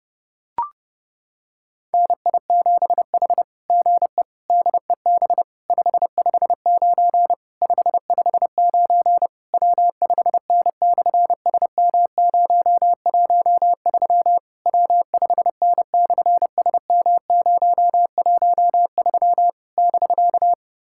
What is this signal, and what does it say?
Morse code, heard as a keyed tone, about 30 wpm, 700 Hz. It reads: NI7H GE DEB 559 559 W5N/SM013 W5N/SM013 <BK>
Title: SOTA QSO Practice 30wpm